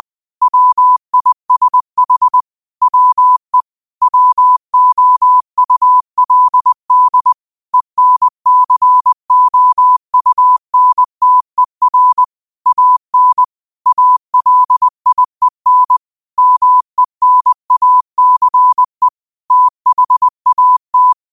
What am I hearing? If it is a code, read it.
Morse code, heard as a keyed tone, about 20 wpm, 1000 Hz. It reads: WISH WE WOULD ENCOUNTER AN ALIEN MENACE THAT
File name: qotdfiles / 20wpm